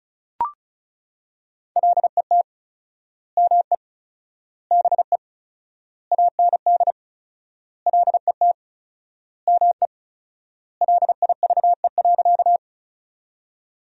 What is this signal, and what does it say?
Morse code, heard as a keyed tone, about 36 wpm, 700 Hz. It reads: LET ME BE AND LET ME LIVE.
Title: 4x Word-Spacing - Sentences from Top 300 Words - 35wpm